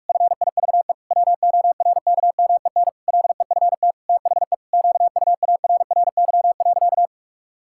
Morse code, heard as a keyed tone, about 46 wpm, 700 Hz. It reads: FIVE WORKMEN LEFT THE QUARRY.
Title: Pangram Challenge v2 - 45wpm